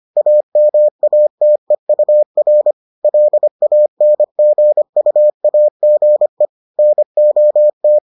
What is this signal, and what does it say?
Morse code, heard as a keyed tone, about 25 wpm, 600 Hz. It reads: AMATEUR LANGUAGE NOT